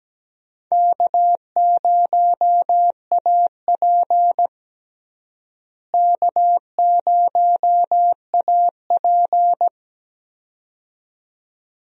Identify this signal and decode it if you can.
Morse code, heard as a keyed tone, about 17 wpm, 700 Hz. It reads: K0AP K0AP
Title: USA Call Signs - 4 Characters - Repeated 2x - 17wpm